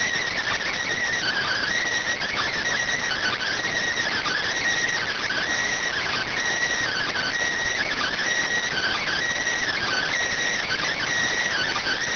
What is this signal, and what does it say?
Emergency Alert System (SAME) tones (header bursts), heard as harsh buzzing bursts. It is heard from 1.0 to 11.6 s.